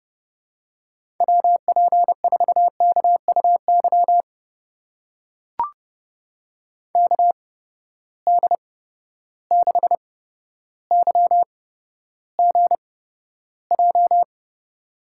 Morse code, heard as a keyed tone, about 31 wpm, spaced as for 13 wpm, 700 Hz. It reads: WP4KUY KD6YGJ
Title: USA Call Signs - 8x Character Spacing - 30wpm